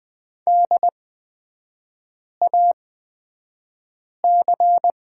Morse code, heard as a keyed tone, about 20 wpm, spaced as for 5 wpm, 700 Hz. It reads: DAC
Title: 243 Lesson 39e - Letter z - Three Characters 20wpm